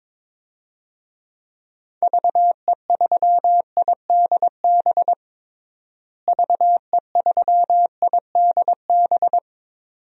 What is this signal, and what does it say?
Morse code, heard as a keyed tone, about 22 wpm, 700 Hz. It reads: VE3IDB VE3IDB
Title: DXCC-100 Call Signs - 6 Characters - Repeated 2x - 22wpm